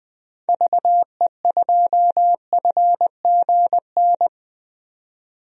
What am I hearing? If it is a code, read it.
Morse code, heard as a keyed tone, about 20 wpm, 700 Hz. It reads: VE2FGN
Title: DXCC-100 Call Signs - 6 Characters - 20wpm